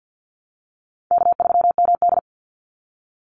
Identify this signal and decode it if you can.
Morse code, heard as a keyed tone, about 50 wpm, 700 Hz. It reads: K3MD
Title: Easy Call Signs - 50wpm